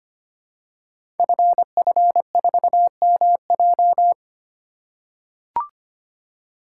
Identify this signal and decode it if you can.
Morse code, heard as a keyed tone, about 25 wpm, 700 Hz. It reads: FF4MJ
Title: Code Groups - 5 Letter:Numbers - 25wpm